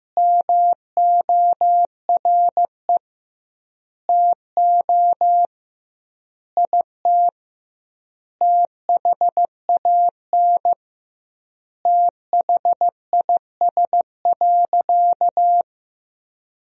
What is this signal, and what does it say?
Morse code, heard as a keyed tone, about 15 wpm, 700 Hz. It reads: MORE TO IT THAN THIS.
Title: 2x Word-Spacing - Sentences from Top 100 Words - 15wpm